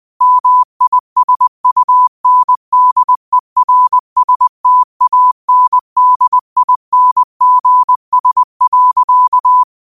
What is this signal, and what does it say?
Morse code, heard as a keyed tone, about 20 wpm, 1000 Hz. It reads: MISUNDERSTANDINGS.
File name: 20wpm